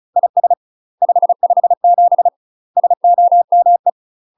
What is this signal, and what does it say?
Morse code, heard as a keyed tone, about 34 wpm, 700 Hz. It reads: IS 557 SOME
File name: UK MASTER QSOs TRAINING / 35WPM / UK MASTER QSOs 0033